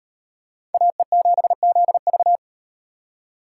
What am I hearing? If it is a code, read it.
Morse code, heard as a keyed tone, about 35 wpm, 700 Hz. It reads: AE7ZV